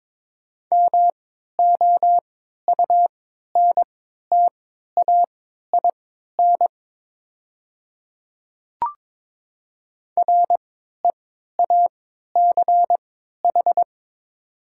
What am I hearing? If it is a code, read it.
Morse code, heard as a keyed tone, about 22 wpm, spaced as for 12 wpm, 690 Hz. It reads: MOUNTAIN REACH
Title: 3x Character Spacing - Top 500 Words - 22wpm